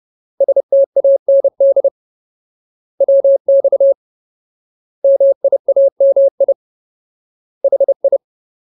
Morse code, heard as a keyed tone, about 30 wpm, 550 Hz. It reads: STAND WX MIAMI HI
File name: Ultimate Word List / 30WPM Ultimate